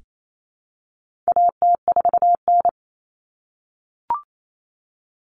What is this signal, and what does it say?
Morse code, heard as a keyed tone, about 28 wpm, 700 Hz. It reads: AT4N
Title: Common Cut Numbers - 4-Digit 28wpm